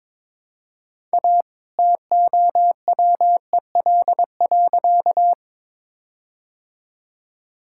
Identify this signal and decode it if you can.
Morse code, heard as a keyed tone, about 22 wpm, 700 Hz. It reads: A TOWEL.
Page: morse-code-ninja-practice-22wpm